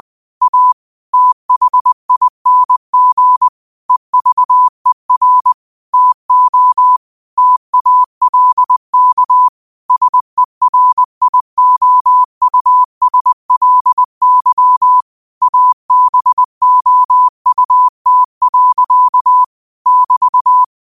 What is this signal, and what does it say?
Morse code, heard as a keyed tone, about 20 wpm, 1000 Hz. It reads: A THING EVER TO TALK SERIOUSLY ABOUT. <BT>
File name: qotdfiles / 20wpm